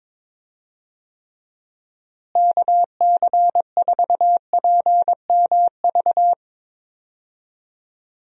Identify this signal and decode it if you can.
Morse code, heard as a keyed tone, about 21 wpm, 700 Hz. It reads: KC4PMV
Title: USA Call Signs - 6 Characters - Repeated 3x - 22wpm